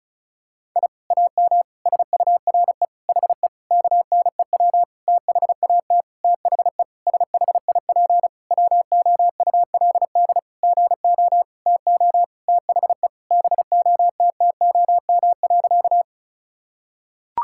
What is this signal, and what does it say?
Morse code, heard as a keyed tone, about 36 wpm, 700 Hz. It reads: I AM SURE HE KNEW THAT THE SHIP WOULD GO TO THE BOTTOM.
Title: Sentences from Top 1000 Words - Review Part 6 - 35wpm